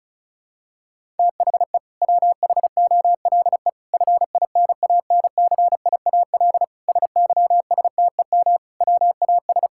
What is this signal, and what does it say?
Morse code, heard as a keyed tone, about 34 wpm, 700 Hz. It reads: THE WHOLE FINANCIAL SYSTEM WAS
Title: Sentences from Top 1000 Words - Review Part 5 - 35wpm